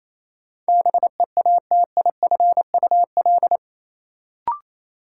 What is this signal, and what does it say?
Morse code, heard as a keyed tone, about 28 wpm, 700 Hz. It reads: BEATIFUL
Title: Sentences from Top 1300 Words - New Word Drill Alt - 28wpm